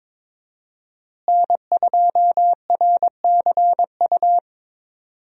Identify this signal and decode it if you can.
Morse code, heard as a keyed tone, about 22 wpm, 700 Hz. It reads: N2RCU
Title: USA Call Signs - 5 Characters - Repeated 2x - 22wpm